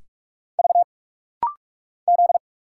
Morse code, heard as a keyed tone, about 46 wpm, 700 Hz. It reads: V Z
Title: Single Letters - Rapid-Fire 45wpm